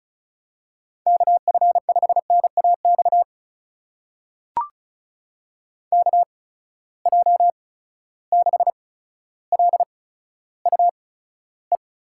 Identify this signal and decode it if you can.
Morse code, heard as a keyed tone, about 36 wpm, spaced as for 15 wpm, 700 Hz. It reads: KF5NAX KJ6LUE
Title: USA Call Signs - 6 Characters - 8x Character Spacing - 35wpm